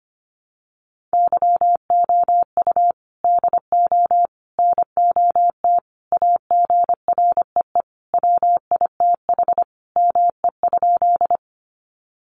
Morse code, heard as a keyed tone, about 25 wpm, 700 Hz. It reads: YOU DO NOT AGREE WITH ME?
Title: Sentences from Top 1100 Words 25wpm